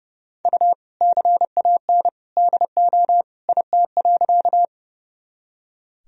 Morse code, heard as a keyed tone, about 30 wpm, 700 Hz. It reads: U CAN DO IT.
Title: Sentences from Top 100 Words and Abbreviations 30wpm